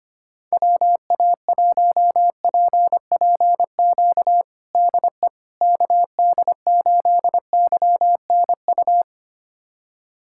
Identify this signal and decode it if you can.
Morse code, heard as a keyed tone, about 25 wpm, 700 Hz. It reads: WA1PPQ DE KD8YNU
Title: USA Callsign DE Callsign - 6 Characters - 25wpm